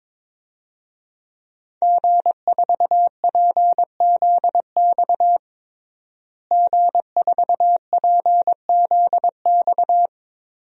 Morse code, heard as a keyed tone, about 22 wpm, 700 Hz. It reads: G4PZX G4PZX
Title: DXCC-100 Call Signs - 5 Characters - Repeated 3x - 22wpm